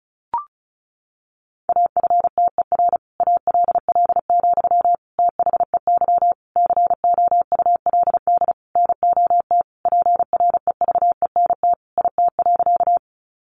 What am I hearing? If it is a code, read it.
Morse code, heard as a keyed tone, about 34 wpm, 700 Hz. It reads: AFTER ALL, THEY COULD NOT PREVENT IT.
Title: Sentences from Top 1100 Words 35wpm